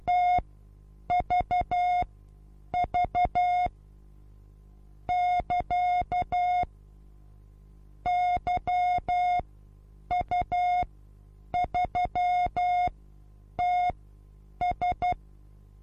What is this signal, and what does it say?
Morse code, heard as a keyed tone, about 12 wpm, spaced as for 8 wpm, 720 Hz. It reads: TVV <KA> YU3TS
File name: repos / Morsekurs / DJ4UF Morsen23